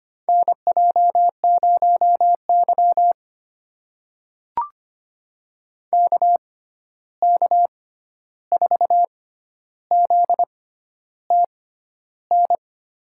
Morse code, heard as a keyed tone, about 25 wpm, spaced as for 11 wpm, 700 Hz. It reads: NJ0Y KK4ZTN
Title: USA Call Signs - 6x Character Spacing - 25wpm